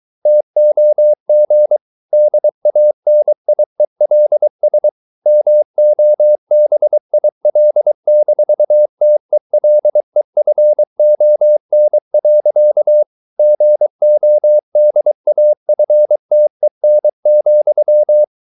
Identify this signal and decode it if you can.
Morse code, heard as a keyed tone, about 23 wpm, 600 Hz. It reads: TOG DANIELS MOBIL-TELEFON. GODAFTEN,
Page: Ferie 23 wpm